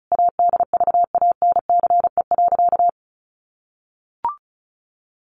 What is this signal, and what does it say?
Morse code, heard as a keyed tone, about 34 wpm, 700 Hz. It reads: ADVANCE.
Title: Sentences from Top 1400 Words 35wpm